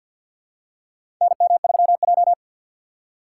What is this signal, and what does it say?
Morse code, heard as a keyed tone, about 48 wpm, 700 Hz. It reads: NM3J
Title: USA Call Signs - 4 Characters - 2x Character Spacing - 50wpm